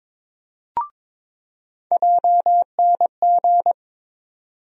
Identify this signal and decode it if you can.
Morse code, heard as a keyed tone, about 22 wpm, 700 Hz. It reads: JNG
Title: USA Call Sign Postfixes - 3 Letters 22wpm